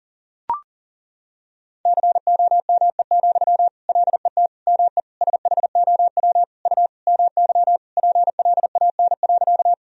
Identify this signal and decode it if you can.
Morse code, heard as a keyed tone, about 37 wpm, 700 Hz. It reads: COME, LET ME SHOW U MY PLAN.
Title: Sentences from Top 500 Words and Abbreviations 40wpm